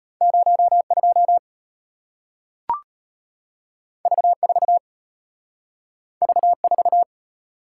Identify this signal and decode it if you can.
Morse code, heard as a keyed tone, about 34 wpm, 700 Hz. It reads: 02 V4 V4